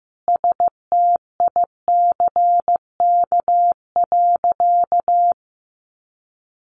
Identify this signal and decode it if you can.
Morse code, heard as a keyed tone, about 15 wpm, 700 Hz. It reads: STICK.
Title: Sentences from Top 1200 Words 15wpm